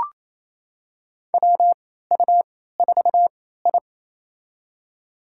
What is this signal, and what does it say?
Morse code, heard as a keyed tone, about 28 wpm, spaced as for 16 wpm, 700 Hz. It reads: WU4I
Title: USA Call Signs - 4 Characters - 3x Character Spacing - 28wpm